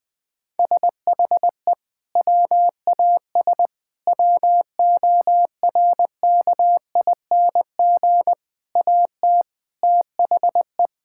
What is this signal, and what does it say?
Morse code, heard as a keyed tone, about 20 wpm, 700 Hz. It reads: SHE WAS WORKING AT THE
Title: Sentences from Top 1000 Words - Review Part 5 - 20wpm